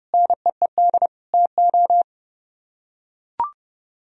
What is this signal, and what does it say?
Morse code, heard as a keyed tone, about 28 wpm, 700 Hz. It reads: NEED TO